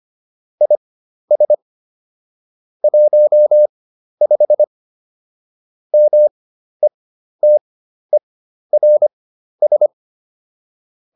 Morse code, heard as a keyed tone, about 25 wpm, spaced as for 12 wpm, 600 Hz. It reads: IS 15 METERS